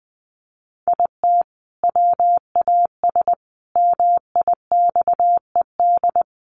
Morse code, heard as a keyed tone, about 20 wpm, 700 Hz. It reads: IT WAS MIXED
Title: Sentences from Top 1500 Words 20wpm